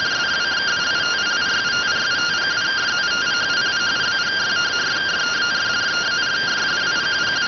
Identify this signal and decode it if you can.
Two-tone teleprinter signal, about 170 Hz shift.